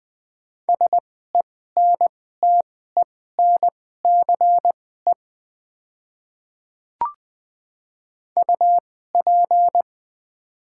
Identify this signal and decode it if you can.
Morse code, heard as a keyed tone, about 20 wpm, spaced as for 14 wpm, 700 Hz. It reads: SENTENCE UP
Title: 2x Character Spacing - Top 200 Words - 20wpm